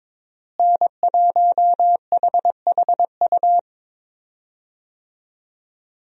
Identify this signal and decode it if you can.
Morse code, heard as a keyed tone, about 22 wpm, 700 Hz. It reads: N1HHU
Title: USA Call Signs - 5 Characters - Repeated 3x - 22wpm